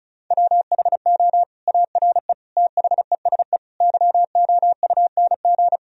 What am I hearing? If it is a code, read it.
Morse code, heard as a keyed tone, about 36 wpm, 700 Hz. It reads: WHO ARE THESE YOUNG
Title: Sentences from Top 1000 Words - Review Part 1 - 35wpm